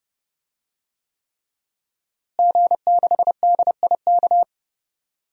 Morse code, heard as a keyed tone, about 32 wpm, 700 Hz. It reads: G6DIK
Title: DXCC-100 Call Signs - 5 Characters - Repeated 3x - 30wpm